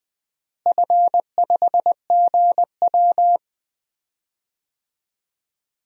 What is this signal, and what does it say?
Morse code, heard as a keyed tone, about 20 wpm, 700 Hz. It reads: F5GW
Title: DXCC-100 Call Signs - 4 Characters - Repeated 2x - 20wpm